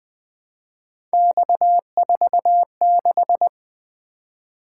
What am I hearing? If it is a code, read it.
Morse code, heard as a keyed tone, about 20 wpm, 700 Hz. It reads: X46
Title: Three Letter-Number Combinations - 20wpm